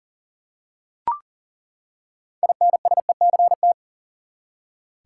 Morse code, heard as a keyed tone, about 40 wpm, 700 Hz. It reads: INSECT
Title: Six Letter Words - 40wpm